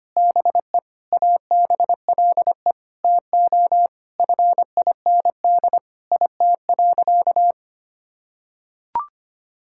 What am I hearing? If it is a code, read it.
Morse code, heard as a keyed tone, about 25 wpm, 700 Hz. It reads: BE ABLE TO FIND IT.
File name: practice / 25wpm / Sentences with 4-Word Phrases - 25wpm